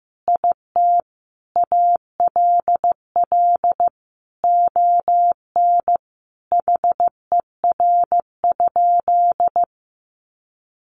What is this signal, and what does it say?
Morse code, heard as a keyed tone, about 14 wpm, 700 Hz. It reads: IT ALL ON HER?
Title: Sentences from Top 800 Words 15wpm